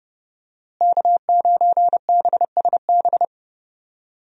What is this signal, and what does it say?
Morse code, heard as a keyed tone, about 30 wpm, 700 Hz. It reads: K9BSB
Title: USA Call Signs - 5 Characters - Repeated 3x - 30wpm